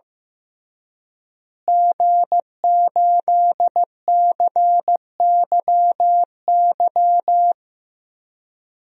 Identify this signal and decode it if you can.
Morse code, heard as a keyed tone, about 15 wpm, 700 Hz. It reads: G8CYY
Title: DXCC-100 Call Signs - 5 Characters - Repeated 3x - 15wpm